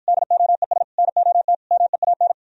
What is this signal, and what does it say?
Morse code, heard as a keyed tone, about 54 wpm, 700 Hz. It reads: DOES NOT MEAN